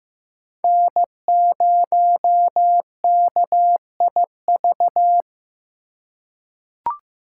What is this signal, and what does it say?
Morse code, heard as a keyed tone, about 15 wpm, 700 Hz. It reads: N0KIV
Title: USA Call Signs - 5 Characters - 2x Character Spacing - 15wpm